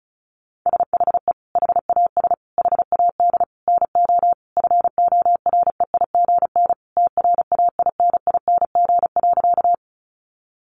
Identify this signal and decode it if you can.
Morse code, heard as a keyed tone, about 34 wpm, 700 Hz. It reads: SHE HAS HAD NO FOREIGN TRAINING.
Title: Sentences from Top 1400 Words 35wpm